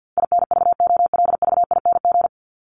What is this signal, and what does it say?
Morse code, heard as a keyed tone, about 50 wpm, 700 Hz. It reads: INVOLVING
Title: ING Suffix - Top 100 Words with Base Words 50wpm